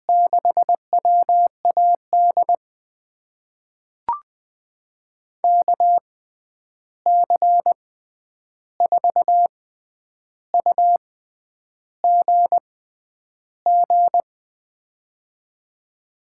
Morse code, heard as a keyed tone, about 20 wpm, spaced as for 9 wpm, 700 Hz. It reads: 6WAD KC4UGG
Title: USA Call Signs - 6 Characters - 6x Character Spacing - 20wpm